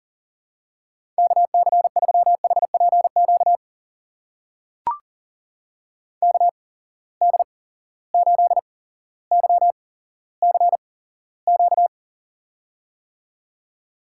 Morse code, heard as a keyed tone, about 40 wpm, spaced as for 17 wpm, 700 Hz. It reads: KC3HPQ KD8YCQ
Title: USA Call Signs - 6 Characters - 8x Character Spacing - 40wpm